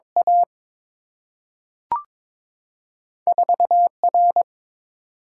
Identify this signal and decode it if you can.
Morse code, heard as a keyed tone, about 22 wpm, 700 Hz. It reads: A 4 R
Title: Single-Letter Single-Number 22wpm